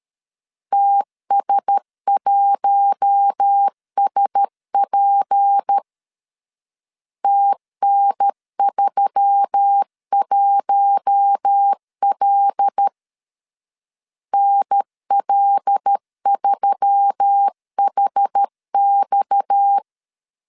Morse code, heard as a keyed tone, about 13 wpm, 800 Hz. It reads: TS1SP TN31L NL3HX
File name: LEZIONI / 6 PKX0 / 6 PKX0 12wpm 2231